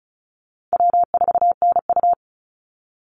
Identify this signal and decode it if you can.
Morse code, heard as a keyed tone, about 34 wpm, 700 Hz. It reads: W4NU
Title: Call Signs - 35wpm